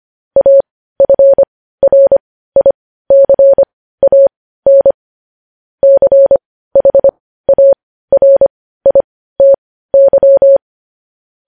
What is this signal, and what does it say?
Morse code, heard as a keyed tone, about 25 wpm, spaced as for 15 wpm, 550 Hz. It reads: AFRICAN CHARITY